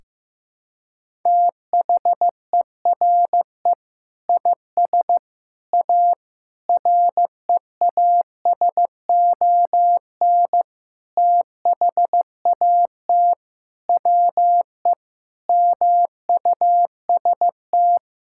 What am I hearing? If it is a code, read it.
Morse code, heard as a keyed tone, about 16 wpm, 700 Hz. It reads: THERE IS A REASON THAT WE MUST
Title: Sentences from Top 1000 Words - Review Part 5 - 15wpm